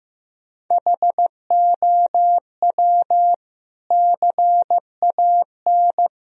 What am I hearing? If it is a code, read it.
Morse code, heard as a keyed tone, about 15 wpm, 700 Hz. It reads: HOW CAN
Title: Sentences from Top 1000 Words - Review Part 1 - 15wpm